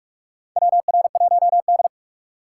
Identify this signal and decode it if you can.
Morse code, heard as a keyed tone, about 44 wpm, 700 Hz. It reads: WR1D
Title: USA Call Signs - 4 Characters - 4x Character Spacing - 45wpm